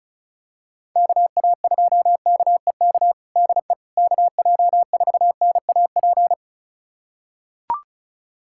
Morse code, heard as a keyed tone, about 36 wpm, 700 Hz. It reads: KA2KEK DE KJ4NAP
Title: USA Callsign DE Callsign - 6 Characters - 35wpm